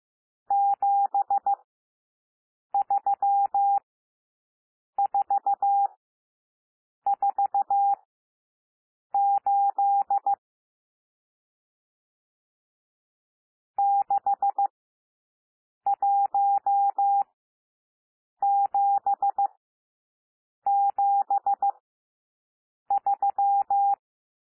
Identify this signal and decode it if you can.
Morse code, heard as a keyed tone, about 15 wpm, spaced as for 6 wpm, 800 Hz. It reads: 73448 61773
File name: cwmp3 / N1 SOUND NUMBERS 15 wpm